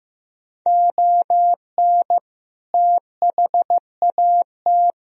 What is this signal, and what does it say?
Morse code, heard as a keyed tone, about 16 wpm, 700 Hz. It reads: ON THAT
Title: Sentences from Top 1000 Words - Review Part 4 - 15wpm